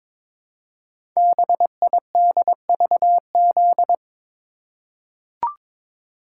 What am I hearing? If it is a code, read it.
Morse code, heard as a keyed tone, about 23 wpm, 700 Hz. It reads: BIDVZ
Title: Code Groups - 5 Letter:Numbers - 22wpm